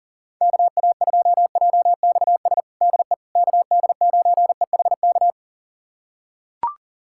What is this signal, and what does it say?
Morse code, heard as a keyed tone, about 40 wpm, 700 Hz. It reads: KA2JXS DE KD9EHK
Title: USA Callsign DE Callsign - 6 Characters - 40wpm